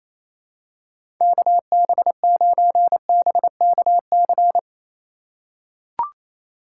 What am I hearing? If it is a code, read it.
Morse code, heard as a keyed tone, about 28 wpm, 700 Hz. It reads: KB9BKC
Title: USA Call Signs - 6 Characters - 4x Character Spacing - 28wpm